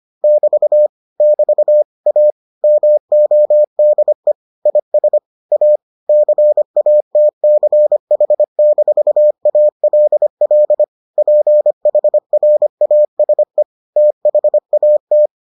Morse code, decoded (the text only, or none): <BT> <BT> A MODE IS A CATCH-ALL PHRASE THAT